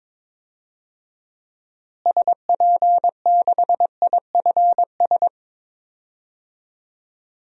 Morse code, heard as a keyed tone, about 22 wpm, 700 Hz. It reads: SP6IFS
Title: DXCC-100 Call Signs - 6 Characters - Repeated 2x - 22wpm